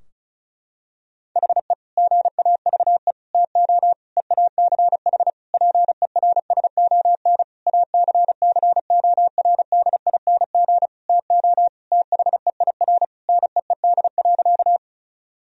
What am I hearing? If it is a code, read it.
Morse code, heard as a keyed tone, about 36 wpm, 700 Hz. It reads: HE GAVE TO EACH PERSON ACCORDING TO THEIR NEED.
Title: Sentences from Top 1000 Words - Review Part 4 - 35wpm